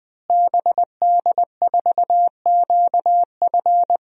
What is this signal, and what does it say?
Morse code, heard as a keyed tone, about 20 wpm, 700 Hz. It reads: BD4QF
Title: DXCC-100 Call Signs - 5 Characters - Repeated 3x - 20wpm